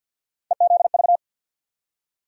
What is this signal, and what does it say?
Morse code, heard as a keyed tone, about 53 wpm, 700 Hz. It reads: EZV
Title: USA Call Sign Postfixes - 3 Letters 50wpm